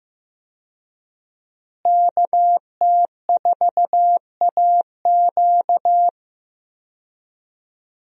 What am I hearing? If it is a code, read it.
Morse code, heard as a keyed tone, about 15 wpm, 700 Hz. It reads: KT4AQ
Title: USA Call Signs - 5 Characters - Repeated 3x - 15wpm